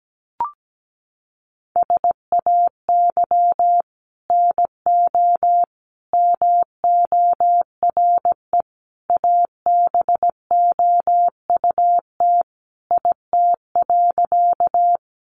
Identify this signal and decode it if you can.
Morse code, heard as a keyed tone, about 17 wpm, 700 Hz. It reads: SAY NO MORE ABOUT IT.
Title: Sentences from Top 200 Words 17wpm